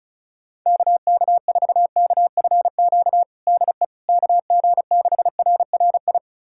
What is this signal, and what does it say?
Morse code, heard as a keyed tone, about 36 wpm, 700 Hz. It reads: KK4KFQ DE KG6RRI
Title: USA Callsign DE Callsign - 6 Characters - 35wpm